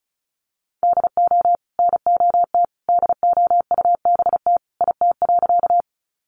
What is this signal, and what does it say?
Morse code, heard as a keyed tone, about 34 wpm, 700 Hz. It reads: DO NOT DOUBT IT.
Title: Sentences from Top 1500 Words 35wpm